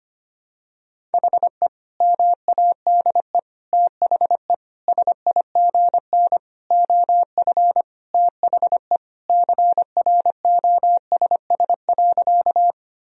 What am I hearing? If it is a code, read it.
Morse code, heard as a keyed tone, about 25 wpm, 700 Hz. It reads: HE MADE THE SIGN OF THE CROSS.
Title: Sentences from Top 1000 Words - Review Part 1 - 25wpm